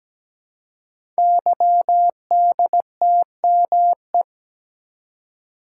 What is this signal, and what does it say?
Morse code, heard as a keyed tone, about 17 wpm, 700 Hz. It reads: YDTME
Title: Code Groups - 5 Letters - 17wpm